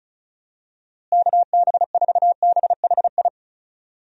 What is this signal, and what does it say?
Morse code, heard as a keyed tone, about 36 wpm, 700 Hz. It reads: KB4BHI
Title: USA Call Signs - 6 Characters - Repeated 2x - 35wpm